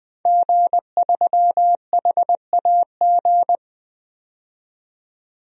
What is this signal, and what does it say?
Morse code, heard as a keyed tone, about 20 wpm, 700 Hz. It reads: G3HAG
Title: DXCC-100 Call Signs - 5 Characters - 20wpm